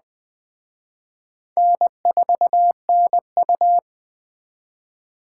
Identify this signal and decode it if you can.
Morse code, heard as a keyed tone, about 20 wpm, 700 Hz. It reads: N4NU